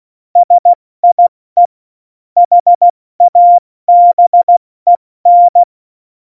Morse code, heard as a keyed tone, about 16 wpm, 700 Hz. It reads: SIE HABEN
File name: Morsekurs / Witze / Witz 20. Februar 2018, 70 F80